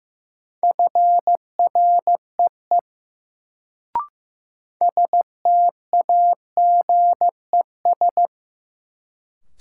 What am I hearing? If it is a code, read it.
Morse code, heard as a keyed tone, about 15 wpm, 700 Hz. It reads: FREE STAGES
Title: Sets of 5 Words Encore - New Word Drill - 15wpm